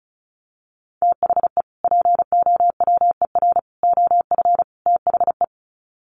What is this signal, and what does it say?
Morse code, heard as a keyed tone, about 34 wpm, 700 Hz. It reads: THE POWER OF THE
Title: Sets of 4 Words 35wpm